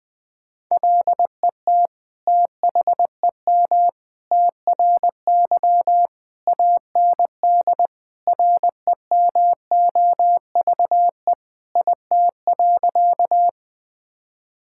Morse code, heard as a keyed tone, about 21 wpm, 700 Hz. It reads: LET THEM TRY AND REMOVE IT.
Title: Sentences from Top 1000 Words - Review Part 5 - 20wpm